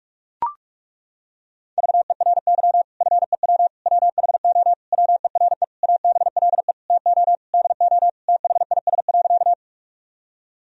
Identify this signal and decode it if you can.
Morse code, heard as a keyed tone, about 41 wpm, 700 Hz. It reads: VERY FEW WHO WERE ABLE TO DO THIS.
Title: Sentences from Top 1000 Words - Review Part 2 - 45wpm